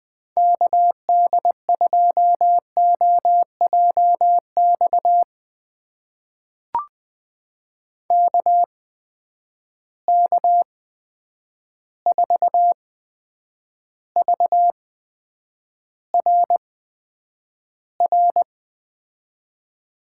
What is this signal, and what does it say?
Morse code, heard as a keyed tone, about 20 wpm, spaced as for 9 wpm, 700 Hz. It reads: KD2OJX KK4VRR
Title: USA Call Signs - 6 Characters - 8x Character Spacing - 20wpm